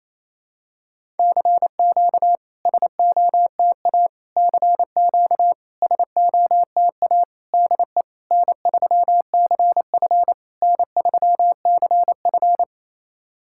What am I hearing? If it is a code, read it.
Morse code, heard as a keyed tone, about 28 wpm, 700 Hz. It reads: CQ SOTA CQ SOTA DE N3CF N3CF
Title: SOTA QSO Practice 28wpm